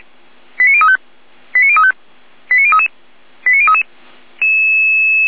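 Five-tone selective call (ZVEI), heard as quick runs of five tones, tones of about 70 ms. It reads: 89035 89035 89033 89033
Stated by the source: ZVEI five-tone selective call received on an SDR.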